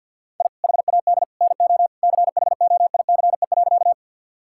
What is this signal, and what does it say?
Morse code, heard as a keyed tone, about 47 wpm, 700 Hz. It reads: I HAD NO CHOICE.
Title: Sentences from Top 1000 Words - Review Part 6 - 50wpm